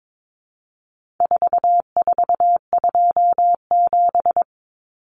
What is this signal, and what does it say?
Morse code, heard as a keyed tone, about 22 wpm, 700 Hz. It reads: 4427
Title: Numbers - 4-Digit 22wpm